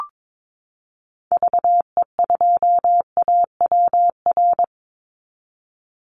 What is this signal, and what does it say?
Morse code, heard as a keyed tone, about 22 wpm, 700 Hz. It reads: VE2AWR
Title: Call Signs Repeated 3x - 22wpm